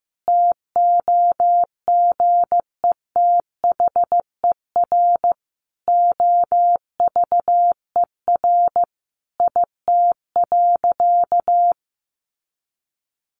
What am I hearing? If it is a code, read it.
Morse code, heard as a keyed tone, about 15 wpm, 700 Hz. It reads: TOGETHER OVER IT.
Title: Sentences from Top 1500 Words 15wpm